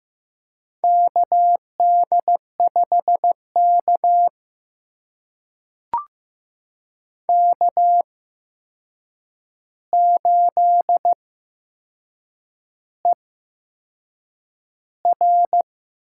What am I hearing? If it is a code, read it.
Morse code, heard as a keyed tone, about 15 wpm, spaced as for 6 wpm, 700 Hz. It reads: KD5K K8ER
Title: USA Call Signs - 4 Characters - 8x Character Spacing - 15wpm